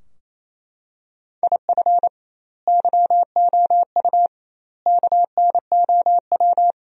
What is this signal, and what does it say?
Morse code, heard as a keyed tone, about 28 wpm, 700 Hz. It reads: IF YOU KNOW
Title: 2x Word-Spacing - Sentences from Top 400 Words - 28wpm